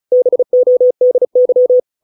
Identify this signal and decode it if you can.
Morse code, heard as a keyed tone, about 34 wpm, 500 Hz. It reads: BODY